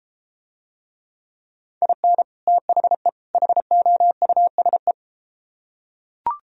Morse code, heard as a keyed tone, about 31 wpm, 700 Hz. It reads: IN THE HOUSE